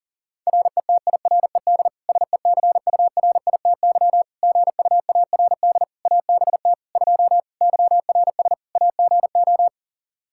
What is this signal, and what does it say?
Morse code, heard as a keyed tone, about 40 wpm, 700 Hz. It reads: RETIRED SECURITY GUARD ABT 2 YRS AGO
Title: Practice QSO - Exchange 3 - More Info - 40wpm